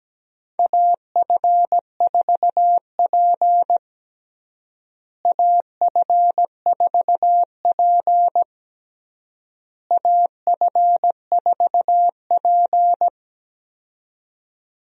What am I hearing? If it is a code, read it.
Morse code, heard as a keyed tone, about 17 wpm, 700 Hz. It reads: AF4P AF4P AF4P
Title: USA Call Signs - 4 Characters - Repeated 3x - 17wpm